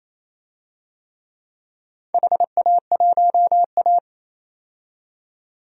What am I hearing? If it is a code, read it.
Morse code, heard as a keyed tone, about 28 wpm, 700 Hz. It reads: HA1A
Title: DXCC-100 Call Signs - 4 Characters - 28wpm